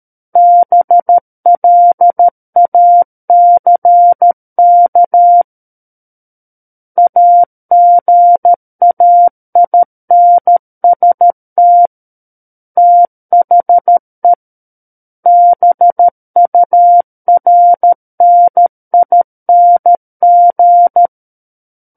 Morse code, heard as a keyed tone, about 13 wpm, 700 Hz. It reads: BLACK AGAINST THE BURNING
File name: Book One / 04-Chapter 4 - 13 WPM